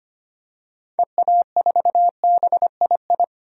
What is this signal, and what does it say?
Morse code, heard as a keyed tone, about 25 wpm, 700 Hz. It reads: EA4BII
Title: DXCC-100 Call Signs - 6 Characters - 25wpm